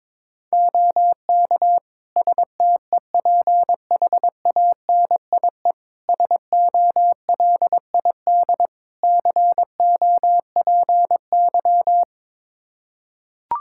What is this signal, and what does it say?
Morse code, heard as a keyed tone, about 22 wpm, 700 Hz. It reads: OK STEPHANIE SOLID COPY
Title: Practice QSO - Exchange 3 - More Info - 22wpm